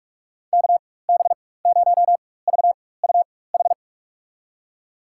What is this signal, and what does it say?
Morse code, heard as a keyed tone, about 46 wpm, spaced as for 21 wpm, 700 Hz. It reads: KB0VUH
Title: USA Call Signs - 6 Characters - 4x Character Spacing - 45wpm